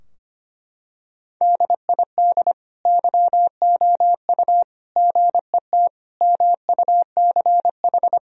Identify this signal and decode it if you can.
Morse code, heard as a keyed tone, about 25 wpm, 700 Hz. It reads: DID YOU GET MUCH
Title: Sentences from Top 1000 Words - Review Part 3 - 25wpm